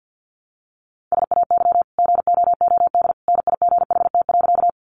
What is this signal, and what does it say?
Morse code, heard as a keyed tone, about 50 wpm, 700 Hz. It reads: SAY GOOD NIGHT.
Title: Sentences from Top 300 Words 50wpm